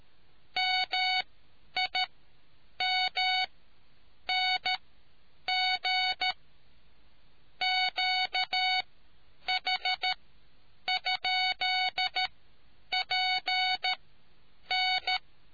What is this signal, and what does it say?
Morse code, heard as a keyed tone, about 13 wpm, spaced as for 8 wpm, 740 Hz. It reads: MIMNG QH?PN